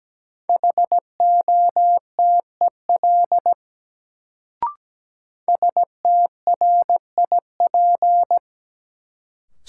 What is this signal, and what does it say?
Morse code, heard as a keyed tone, about 17 wpm, 700 Hz. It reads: HOTEL STRIP
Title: Sentences from Top 1600 Words - New Word Drill Alt - 17wpm